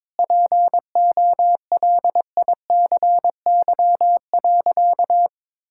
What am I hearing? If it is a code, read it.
Morse code, heard as a keyed tone, about 22 wpm, 700 Hz. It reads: POLICY.
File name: practice / 22wpm / Sentences with 8-Word Phrases - 22wpm